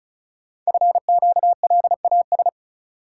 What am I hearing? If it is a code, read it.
Morse code, heard as a keyed tone, about 36 wpm, 700 Hz. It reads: FQLAS
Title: Code Groups - 5 Letters - 35wpm